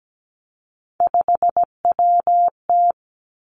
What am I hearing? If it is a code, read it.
Morse code, heard as a keyed tone, about 17 wpm, 700 Hz. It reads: 5WT
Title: Three Challenging Letter-Number Combinations 17wpm